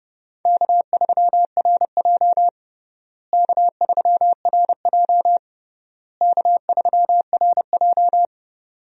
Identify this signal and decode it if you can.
Morse code, heard as a keyed tone, about 30 wpm, 700 Hz. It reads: K3RJ K3RJ K3RJ
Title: USA Call Signs - 4 Characters - Repeated 3x - 30wpm